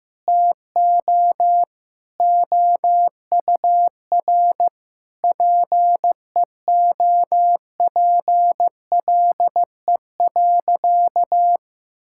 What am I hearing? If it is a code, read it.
Morse code, heard as a keyed tone, about 15 wpm, 700 Hz. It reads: TO OUR PEOPLE.
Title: Sentences from Top 1000 Words - Review Part 6 - 15wpm